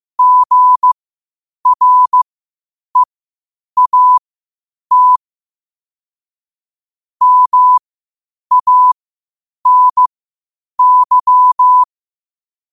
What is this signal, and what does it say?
Morse code, heard as a keyed tone, about 15 wpm, spaced as for 8 wpm, 1000 Hz. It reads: GREAT MANY
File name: qotdfiles / 8wpm